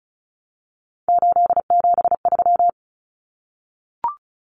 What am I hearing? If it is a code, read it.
Morse code, heard as a keyed tone, about 34 wpm, 700 Hz. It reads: 873
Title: Common Cut Numbers - 3-Digit 35wpm